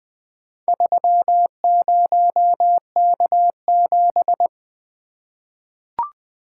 Morse code, heard as a keyed tone, about 20 wpm, 700 Hz. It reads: 30K7
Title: Four Letter-Number Combinations - 20wpm